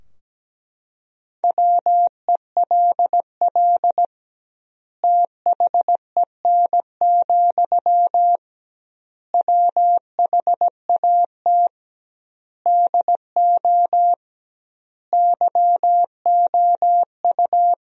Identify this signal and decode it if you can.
Morse code, heard as a keyed tone, about 17 wpm, 700 Hz. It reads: WELL THEN, WHAT DO YOU
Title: 2x Word-Spacing - Sentences from Top 200 Words - 17wpm